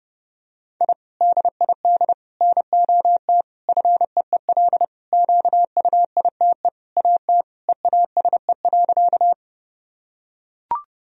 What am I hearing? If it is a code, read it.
Morse code, heard as a keyed tone, about 30 wpm, 700 Hz. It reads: I DID NOT FEEL QUITE AT EASE.
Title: Sentences from Top 1000 Words - Review Part 1 - 30wpm